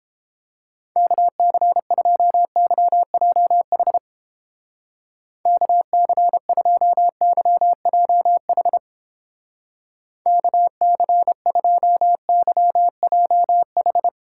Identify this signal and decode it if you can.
Morse code, heard as a keyed tone, about 30 wpm, 700 Hz. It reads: KC2YJH KC2YJH KC2YJH